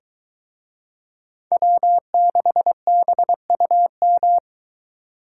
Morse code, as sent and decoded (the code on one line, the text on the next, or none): .-- -.... -... ..- --
W6BUM